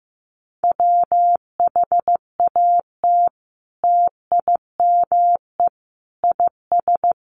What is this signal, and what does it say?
Morse code, heard as a keyed tone, about 15 wpm, 700 Hz. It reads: WHAT TIME IS
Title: Sentences from Top 100 Words 15wpm